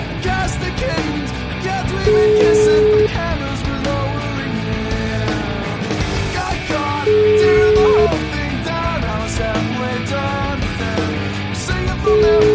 Ringback tone (Europe, 425 Hz), with noise.